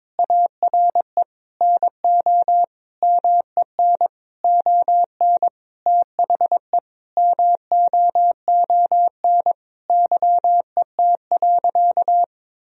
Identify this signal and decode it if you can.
Morse code, heard as a keyed tone, about 22 wpm, 700 Hz. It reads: ARE NO MEN ON THE MOON YET.
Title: Sentences from Top 1000 Words - Review Part 3 - 22wpm